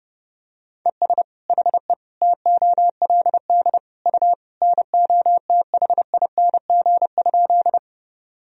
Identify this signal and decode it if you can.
Morse code, heard as a keyed tone, about 30 wpm, 700 Hz. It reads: ES HE TOLD U NOTHING?
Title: Sentences from Top 500 Words and Abbreviations 30wpm